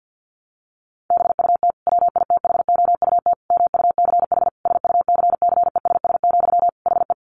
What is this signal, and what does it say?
Morse code, heard as a keyed tone, about 50 wpm, 700 Hz. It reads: BUT WITHOUT MUCH SUCCESS, HE
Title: Sentences from Top 900 Words 50wpm